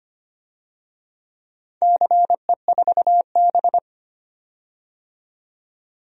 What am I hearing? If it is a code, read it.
Morse code, heard as a keyed tone, about 25 wpm, 700 Hz. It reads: CE4B
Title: DXCC-100 Call Signs - 4 Characters - 25wpm